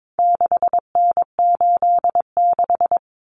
Morse code, heard as a keyed tone, about 22 wpm, 700 Hz. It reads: 6N86
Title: Common Cut Numbers - 4-Digit 22wpm